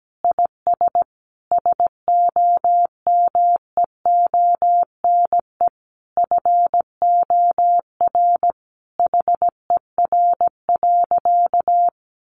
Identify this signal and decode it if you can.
Morse code, heard as a keyed tone, about 17 wpm, 700 Hz. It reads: IS SOMEONE FOR HER.
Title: Sentences from Top 600 Words 17wpm